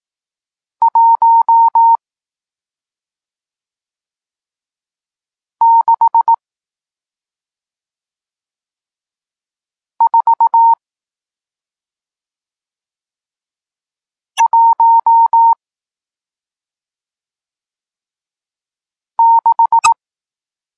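Morse code, heard as a keyed tone, about 18 wpm, spaced as for 2 wpm, 920 Hz. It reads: 16416
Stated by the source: Morse code lesson teaching digits 1, 4 and 6, from SP9MOA, Niepołomicki Klub Krótkofalowców (Poland) amateur radio club.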